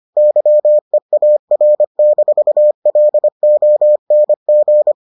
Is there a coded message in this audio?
Yes: Morse code from 0.2 to 4.9 s, heard as a keyed tone, about 25 wpm, 600 Hz. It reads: YEAR-LONG